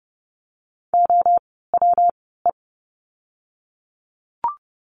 Morse code, heard as a keyed tone, about 30 wpm, 700 Hz. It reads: O W E
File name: course / 30:10wpm / 115 Lesson 19f - Letter w - Words 30wpm